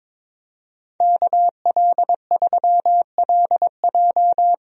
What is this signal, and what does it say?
Morse code, heard as a keyed tone, about 21 wpm, 700 Hz. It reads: KL3LJ
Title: USA Call Signs - 5 Characters - Repeated 2x - 22wpm